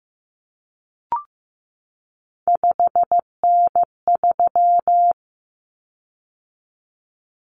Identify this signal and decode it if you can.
Morse code, heard as a keyed tone, about 15 wpm, 700 Hz. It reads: 5N3
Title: Common Cut Numbers - 3-Digit 15wpm